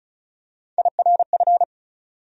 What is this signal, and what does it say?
Morse code, heard as a keyed tone, about 36 wpm, 700 Hz. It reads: IRF
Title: USA Call Sign Postfixes - 3 Letters 35wpm